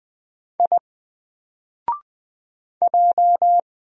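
Morse code, heard as a keyed tone, about 20 wpm, 700 Hz. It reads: I J
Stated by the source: Single letter and number Morse drill at 20 wpm, from N4LCD.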